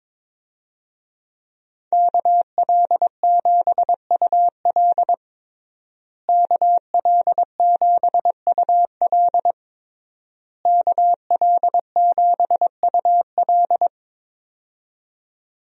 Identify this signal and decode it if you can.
Morse code, heard as a keyed tone, about 22 wpm, 700 Hz. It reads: KL7UL KL7UL KL7UL
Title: USA Call Signs - 5 Characters - Repeated 3x - 22wpm